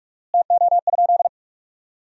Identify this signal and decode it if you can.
Morse code, heard as a keyed tone, about 43 wpm, 700 Hz. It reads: TO?
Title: 4x Word-Spacing - Sentences from Top 300 Words - 45wpm